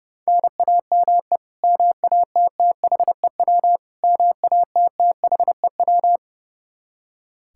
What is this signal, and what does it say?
Morse code, heard as a keyed tone, about 30 wpm, 700 Hz. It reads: NAME MATTHEW MATTHEW
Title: K1USN SST QSO Practice 30wpm